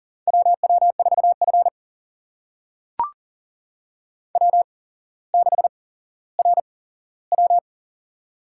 Morse code, heard as a keyed tone, about 42 wpm, spaced as for 17 wpm, 700 Hz. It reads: WW4F W6RW
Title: USA Call Signs - 4 Characters - 8x Character Spacing - 40wpm